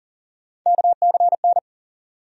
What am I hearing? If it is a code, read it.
Morse code, heard as a keyed tone, about 40 wpm, 700 Hz. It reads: KCN
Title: USA Call Sign Postfixes - 3 Letters 40wpm